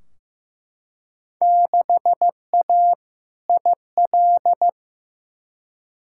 Morse code, heard as a keyed tone, about 15 wpm, 700 Hz. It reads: 6A IL
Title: Field Day Exchanges - 15wpm